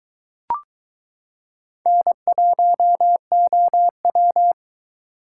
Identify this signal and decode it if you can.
Morse code, heard as a keyed tone, about 22 wpm, 700 Hz. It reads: N1OW